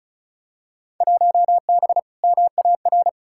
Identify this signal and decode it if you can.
Morse code, heard as a keyed tone, about 34 wpm, 700 Hz. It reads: 1B MAR
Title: Field Day Exchanges Encore - 35wpm